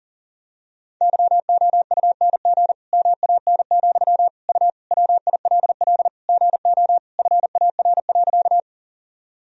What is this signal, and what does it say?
Morse code, heard as a keyed tone, about 40 wpm, 700 Hz. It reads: YOUNG MAN, U WILL GO FAR.
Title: Sentences from Top 400 Words and Abbreviations 40wpm